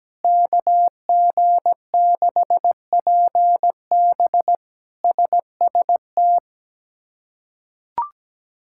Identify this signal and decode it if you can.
Morse code, heard as a keyed tone, about 17 wpm, 700 Hz. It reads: KG6PB SST
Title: K1USN SST QSO Practice - 17wpm